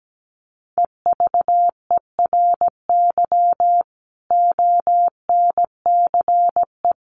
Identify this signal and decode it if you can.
Morse code, heard as a keyed tone, about 17 wpm, 700 Hz. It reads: EVERY ONCE
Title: Sets of 5 Words 17wpm